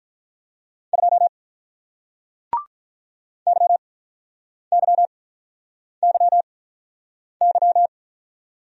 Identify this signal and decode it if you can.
Morse code, heard as a keyed tone, about 48 wpm, spaced as for 9 wpm, 690 Hz. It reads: 2 YYYY